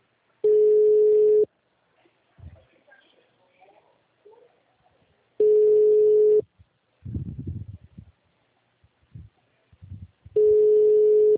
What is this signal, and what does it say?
Ringback tone (Europe, 425 Hz), with speech.